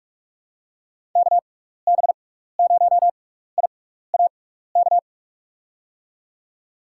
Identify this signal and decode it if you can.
Morse code, heard as a keyed tone, about 46 wpm, spaced as for 16 wpm, 700 Hz. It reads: KB0IAK
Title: USA Call Signs - 6x Character Spacing - 45wpm